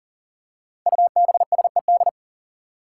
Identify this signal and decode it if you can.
Morse code, heard as a keyed tone, about 40 wpm, 700 Hz. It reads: UBSED
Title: Five Challenging Letter-Number Combinations 40wpm